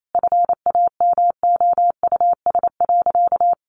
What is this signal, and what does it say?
Morse code, heard as a keyed tone, about 28 wpm, 700 Hz. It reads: FAMOUS.
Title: Sentences from Top 1500 Words 28wpm